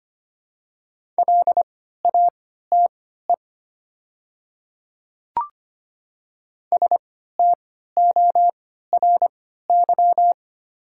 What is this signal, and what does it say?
Morse code, heard as a keyed tone, about 25 wpm, spaced as for 14 wpm, 700 Hz. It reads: LATE STORY
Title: 3x Character Spacing - Top 300 Words - 25wpm